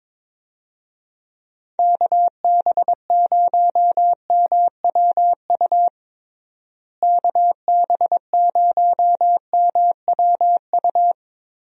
Morse code, heard as a keyed tone, about 22 wpm, 700 Hz. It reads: KB0MWU KB0MWU
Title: USA Call Signs - 6 Characters - Repeated 3x - 22wpm